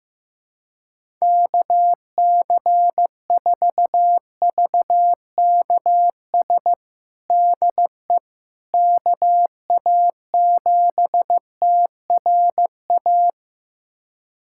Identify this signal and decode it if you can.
Morse code, heard as a keyed tone, about 15 wpm, 700 Hz. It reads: KC4VKS DE KA7TRA
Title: USA Callsign DE Callsign 15wpm